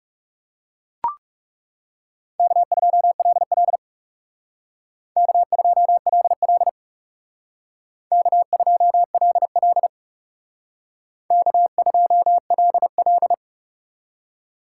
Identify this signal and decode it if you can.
Morse code, heard as a keyed tone, about 35 wpm, 700 Hz. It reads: K2LL K2LL K2LL K2LL
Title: Speed Racing - USA 1x2 Call Signs 30wpm